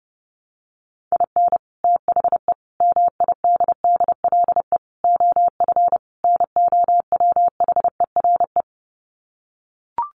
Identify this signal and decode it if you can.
Morse code, heard as a keyed tone, about 30 wpm, 700 Hz. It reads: IN THE MIDDLE OF NOWHERE
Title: Sets of 5 Words Encore 30wpm